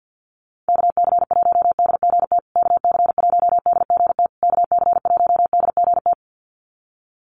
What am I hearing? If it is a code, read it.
Morse code, heard as a keyed tone, about 49 wpm, 700 Hz. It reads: KC1DGT KC1DGT KC1DGT
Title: USA Call Signs Repeated 3x 50wpm